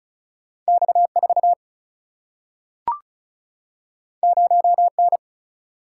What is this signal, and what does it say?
Morse code, heard as a keyed tone, about 36 wpm, 700 Hz. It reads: X4 0N
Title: Single-Letter Single-Number 35wpm